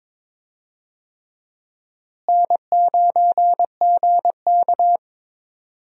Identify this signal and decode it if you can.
Morse code, heard as a keyed tone, about 22 wpm, 700 Hz. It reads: N9GK
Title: USA Call Signs - 4 Characters - Repeated 3x - 22wpm